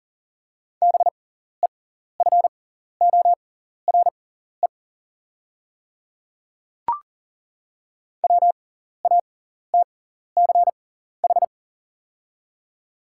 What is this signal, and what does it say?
Morse code, heard as a keyed tone, about 40 wpm, spaced as for 14 wpm, 690 Hz. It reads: BEFORE WATCH
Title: 6x Character Spacing - Top 300 Words - 40wpm